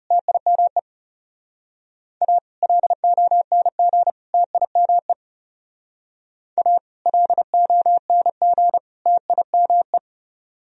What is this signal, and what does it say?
Morse code, heard as a keyed tone, about 36 wpm, 700 Hz. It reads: TIME A LONG TIME A LONG TIME
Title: Speed-Racing - Sets of 3 Words 30wpm